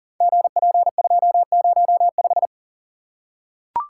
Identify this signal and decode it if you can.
Morse code, heard as a keyed tone, about 39 wpm, 700 Hz. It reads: GP205
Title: Code Groups - 5 Letter:Numbers - 40wpm